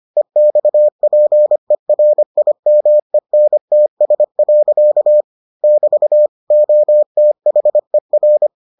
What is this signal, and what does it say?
Morse code, heard as a keyed tone, about 25 wpm, 600 Hz. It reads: EXPERIMENTS. <BT> OTHER